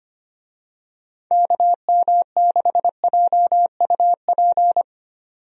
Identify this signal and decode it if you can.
Morse code, heard as a keyed tone, about 25 wpm, 700 Hz. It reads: KM6JUP
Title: USA Call Signs - 6 Characters - 25wpm